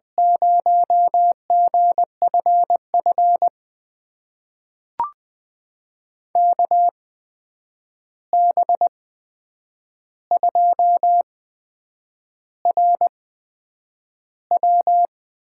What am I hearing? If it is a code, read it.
Morse code, heard as a keyed tone, about 20 wpm, spaced as for 8 wpm, 700 Hz. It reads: 0GFF KB2RW
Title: USA Call Signs - 6 Characters - 8x Character Spacing - 20wpm